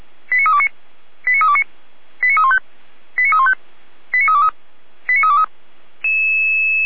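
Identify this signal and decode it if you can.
Five-tone selective call (ZVEI), heard as quick runs of five tones, tones of about 70 ms. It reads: 89329 89329 89315 89315 89323 89323
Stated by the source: ZVEI five-tone selective call received on an SDR.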